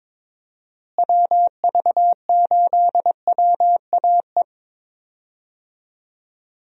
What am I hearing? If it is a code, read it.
Morse code, heard as a keyed tone, about 22 wpm, 700 Hz. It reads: WV8WAE
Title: USA Call Signs - 6 Characters - Repeated 2x - 22wpm